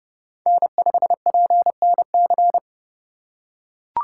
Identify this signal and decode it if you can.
Morse code, heard as a keyed tone, about 31 wpm, 700 Hz. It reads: N5PNC
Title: USA Call Signs - 4x Character Spacing - 30wpm